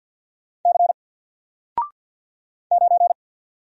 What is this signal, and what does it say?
Morse code, heard as a keyed tone, about 48 wpm, 700 Hz. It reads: C 9
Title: Single Character (Letter:Number:Punctuation:Prosign) Alt 50wpm